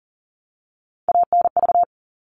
Morse code, heard as a keyed tone, about 40 wpm, 700 Hz. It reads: ANV
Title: All Cut Numbers - 3-Digit 40wpm